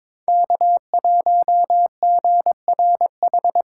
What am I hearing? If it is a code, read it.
Morse code, heard as a keyed tone, about 21 wpm, 690 Hz. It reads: K1GRH
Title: USA Call Signs - 5 Characters - 2x Character Spacing - 22wpm